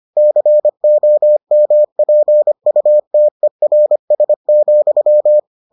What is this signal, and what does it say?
Morse code, heard as a keyed tone, about 25 wpm, 600 Hz. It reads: COMPUTERS,